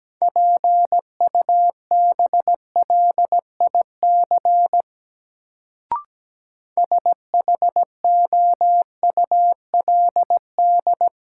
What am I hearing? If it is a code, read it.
Morse code, heard as a keyed tone, about 17 wpm, 700 Hz. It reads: PUBLIC SHOULD
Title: Sets of 2 Words - New Word Drill - 17wpm